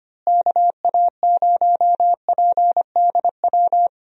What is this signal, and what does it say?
Morse code, heard as a keyed tone, about 25 wpm, 700 Hz. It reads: KA0PDW
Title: USA Call Signs - 6 Characters - Repeated 3x - 25wpm